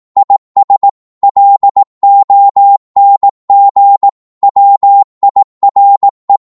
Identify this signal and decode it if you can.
Morse code, heard as a keyed tone, about 18 wpm, 800 Hz. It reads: IS LONG WIRE
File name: Morse / qso-18wpm